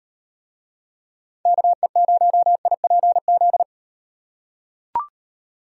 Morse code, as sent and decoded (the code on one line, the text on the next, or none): -.- . ----- .. .--. --..
KE0IPZ